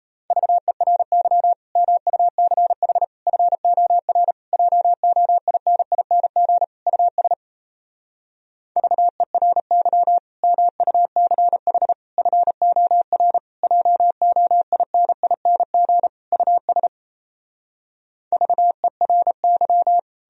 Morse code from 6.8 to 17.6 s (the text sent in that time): US VERY MUCH FOR JOINING US